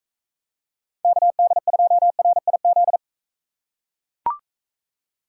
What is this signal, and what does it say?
Morse code, heard as a keyed tone, about 42 wpm, 700 Hz. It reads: KD2RIZ